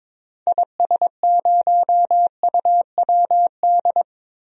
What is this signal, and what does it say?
Morse code, heard as a keyed tone, about 22 wpm, 700 Hz. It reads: IS0UWD
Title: DXCC-100 Call Signs - 6 Characters - Repeated 2x - 22wpm